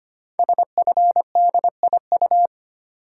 Morse code, heard as a keyed tone, about 25 wpm, 700 Hz. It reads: SFDIU
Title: Code Groups - 5 Letters - 25wpm